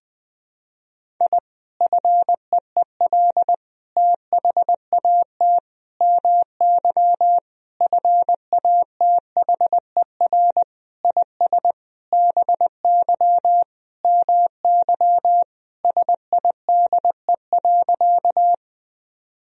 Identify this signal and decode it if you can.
Morse code, heard as a keyed tone, about 21 wpm, 700 Hz. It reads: I FEEL THAT MY FATHER IS BY MY SIDE.
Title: Sentences from Top 1000 Words - Review Part 1 - 20wpm